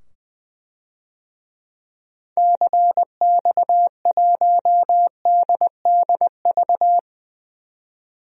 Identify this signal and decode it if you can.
Morse code, heard as a keyed tone, about 21 wpm, 700 Hz. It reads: CX1DDV
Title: DXCC-100 Call Signs - 6 Characters - 20wpm